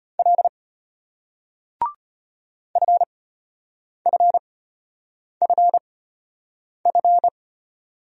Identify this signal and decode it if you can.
Morse code, heard as a keyed tone, about 35 wpm, spaced as for 8 wpm, 700 Hz. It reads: L FFFF